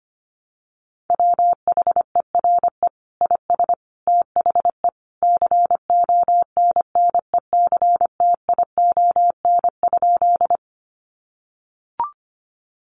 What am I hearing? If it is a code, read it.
Morse code, heard as a keyed tone, about 25 wpm, 700 Hz. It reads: WHERE IS THE CONNECTION?
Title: Sentences from Top 1300 Words 25wpm